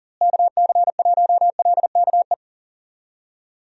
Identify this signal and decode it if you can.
Morse code, heard as a keyed tone, about 40 wpm, 700 Hz. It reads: KC1LKE
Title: USA Call Signs - 6 Characters - Repeated 3x - 40wpm